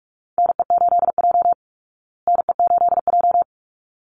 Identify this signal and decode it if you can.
Morse code, heard as a keyed tone, about 44 wpm, 700 Hz. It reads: NE8J NE8J
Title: Call Signs Repeated 3x - 45wpm